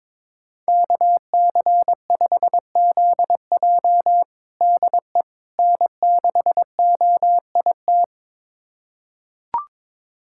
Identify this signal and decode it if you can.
Morse code, heard as a keyed tone, about 22 wpm, 700 Hz. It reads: KC5ZJ DE N6OIT
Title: USA Callsign DE Callsign - 5 Characters - 22wpm